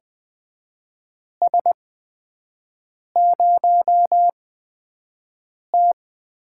Morse code, heard as a keyed tone, about 20 wpm, spaced as for 5 wpm, 700 Hz. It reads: S0T
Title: USA Call Signs - 4 Characters - 8x Character Spacing - 20wpm